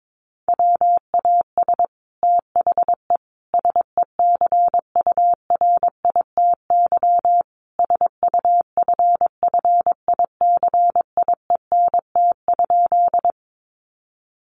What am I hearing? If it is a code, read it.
Morse code, heard as a keyed tone, about 22 wpm, 700 Hz. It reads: WAS THE SECURITY SUFFICIENT?
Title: Sentences from Top 1400 Words 22wpm